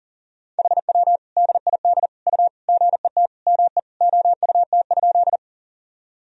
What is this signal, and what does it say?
Morse code, heard as a keyed tone, about 40 wpm, 700 Hz. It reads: HW DID U GET ME OUT?
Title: Sentences from Top 200 Words and Abbreviations 40wpm